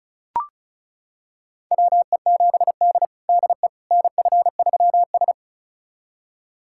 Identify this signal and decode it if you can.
Morse code, heard as a keyed tone, about 36 wpm, 700 Hz. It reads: WE7D DE NF3S
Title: USA Callsign DE Callsign - 4 Characters - 35wpm